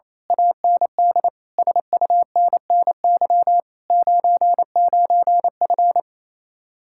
Morse code, heard as a keyed tone, about 26 wpm, 700 Hz. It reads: AND SUNNY 99F
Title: Speed-Racing Common QSO Phrases - 28wpm